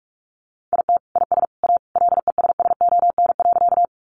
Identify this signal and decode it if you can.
Morse code, heard as a keyed tone, about 44 wpm, 700 Hz. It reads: IT IS A LESSON.
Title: Sentences from Top 1200 Words 45wpm